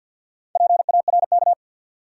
Morse code, heard as a keyed tone, about 53 wpm, 700 Hz. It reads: PARK
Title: Sentences from Top 1300 Words - New Word Drill Alt - 50wpm